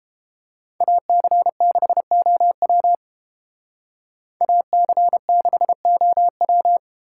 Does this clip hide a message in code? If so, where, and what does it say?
Yes: Morse code from 0.8 to 6.8 s, heard as a keyed tone, about 30 wpm, 700 Hz. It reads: AC6OW AC6OW